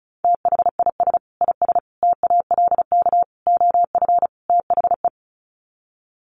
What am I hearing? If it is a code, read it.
Morse code, heard as a keyed tone, about 34 wpm, 700 Hz. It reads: THIS IS TALK OF THE
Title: Sets of 5 Words 35wpm